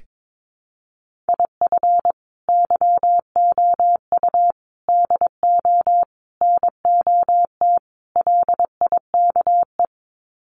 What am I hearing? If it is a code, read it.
Morse code, heard as a keyed tone, about 22 wpm, 700 Hz. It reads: IF YOU DO NOT LIKE
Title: Sets of 5 Words 22wpm